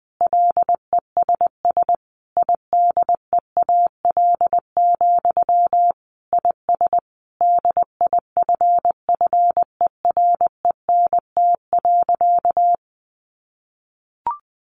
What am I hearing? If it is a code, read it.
Morse code, heard as a keyed tone, about 20 wpm, 700 Hz. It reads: LESS IDEAL, IS DIFFERENT.
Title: Sentences from Top 1400 Words 20wpm